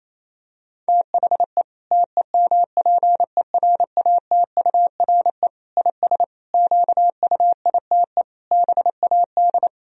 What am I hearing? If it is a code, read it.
Morse code, heard as a keyed tone, about 27 wpm, 700 Hz. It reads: THE TEMPERATURE IS QUITE BAD
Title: Sentences from Top 1000 Words - Review Part 4 - 28wpm